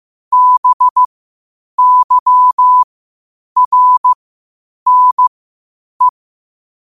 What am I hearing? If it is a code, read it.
Morse code, heard as a keyed tone, about 15 wpm, spaced as for 8 wpm, 1000 Hz. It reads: BYRNE
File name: qotdfiles / 8wpm